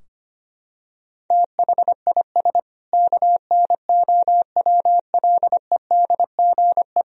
Morse code, heard as a keyed tone, about 25 wpm, 700 Hz. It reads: THIS KNOWLEDGE
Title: Sentences from Top 1000 Words - Review Part 3 - 25wpm